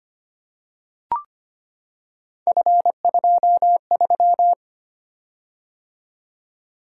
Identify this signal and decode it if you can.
Morse code, heard as a keyed tone, about 25 wpm, 700 Hz. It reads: F23
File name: Three Letter-Number Combinations - 25wpm